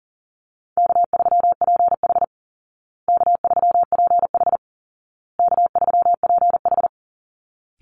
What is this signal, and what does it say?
Morse code, heard as a keyed tone, about 40 wpm, 700 Hz. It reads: K3PH K3PH K3PH
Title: Call Signs Repeated 2x - 40wpm